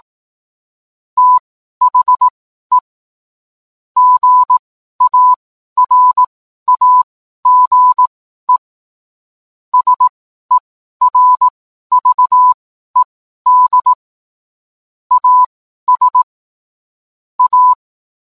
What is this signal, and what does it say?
Morse code, heard as a keyed tone, about 18 wpm, spaced as for 12 wpm, 1000 Hz. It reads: THE GARAGE SERVED AS A